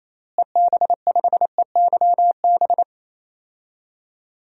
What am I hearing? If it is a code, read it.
Morse code, heard as a keyed tone, about 28 wpm, 700 Hz. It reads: EB5EYB
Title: DXCC-100 Call Signs - 6 Characters - Repeated 2x - 28wpm